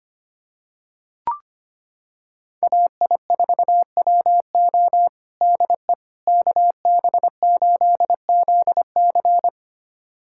Morse code, heard as a keyed tone, about 25 wpm, 700 Hz. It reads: AI4WO DE KB8ZC
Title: USA Callsign DE Callsign - 5 Characters - 25wpm